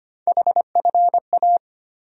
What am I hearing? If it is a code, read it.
Morse code, heard as a keyed tone, about 25 wpm, 700 Hz. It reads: HFA
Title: USA Call Sign Postfixes - 3 Letters 25wpm